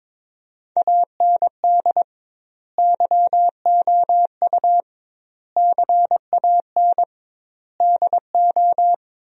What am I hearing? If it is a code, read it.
Morse code, heard as a keyed tone, about 22 wpm, 700 Hz. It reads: AND YOU CAN DO
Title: 2x Word-Spacing - Sentences from Top 500 Words - 22wpm